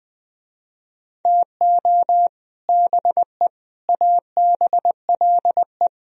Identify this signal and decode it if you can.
Morse code, heard as a keyed tone, about 20 wpm, 700 Hz. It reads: TO BE ABLE